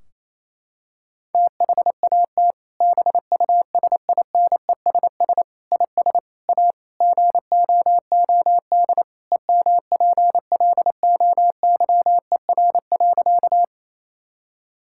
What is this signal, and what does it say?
Morse code, heard as a keyed tone, about 28 wpm, 700 Hz. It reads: THAT BUSINESS IS A GOOD EMPLOYER.
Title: Sentences from Top 1600 Words 28wpm